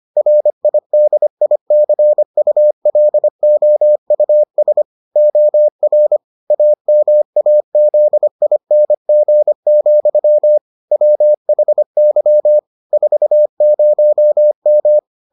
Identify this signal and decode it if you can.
Morse code, heard as a keyed tone, about 25 wpm, 600 Hz. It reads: RIDICULOUS OR AMAZING, WHY 40M